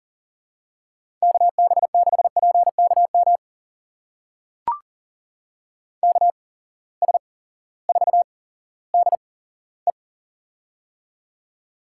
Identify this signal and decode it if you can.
Morse code, heard as a keyed tone, about 42 wpm, spaced as for 19 wpm, 700 Hz. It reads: KB6PKM KS4DE
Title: USA Call Signs - 8x Character Spacing - 40wpm